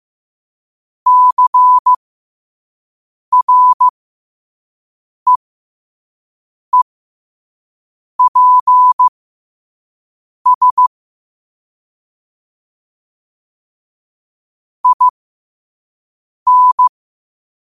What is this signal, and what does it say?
Morse code, heard as a keyed tone, about 15 wpm, spaced as for 5 wpm, 1000 Hz. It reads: CREEPS IN
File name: qotdfiles / 5wpm